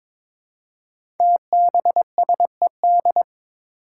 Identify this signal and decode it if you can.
Morse code, heard as a keyed tone, about 22 wpm, 700 Hz. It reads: TBSED
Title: Five Challenging Letter-Number Combinations 22wpm